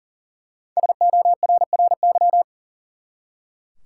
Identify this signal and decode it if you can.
Morse code, heard as a keyed tone, about 40 wpm, 700 Hz. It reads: SORRY
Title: Five Letter Words - 40wpm